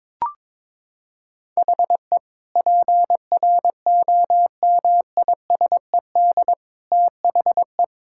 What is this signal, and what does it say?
Morse code, heard as a keyed tone, about 21 wpm, 700 Hz. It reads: HE PROMISED THE
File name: practice / 22wpm / Sentences with 8-Word Phrases - 22wpm